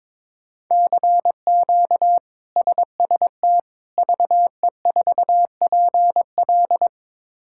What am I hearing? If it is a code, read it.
Morse code, heard as a keyed tone, about 22 wpm, 700 Hz. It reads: CQ SST VE4PL
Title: K1USN SST QSO Practice - 22wpm